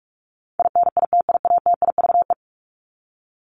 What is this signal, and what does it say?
Morse code, heard as a keyed tone, about 44 wpm, 700 Hz. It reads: INITIATIVE
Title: Top 1751-2000 Words 45wpm